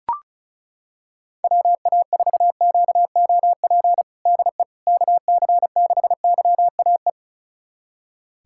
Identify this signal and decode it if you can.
Morse code, heard as a keyed tone, about 36 wpm, 700 Hz. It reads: WA4QOP DE KC6YAE
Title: USA Callsign DE Callsign 35wpm